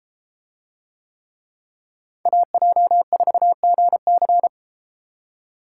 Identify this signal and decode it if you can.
Morse code, heard as a keyed tone, about 33 wpm, 700 Hz. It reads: AJ4GC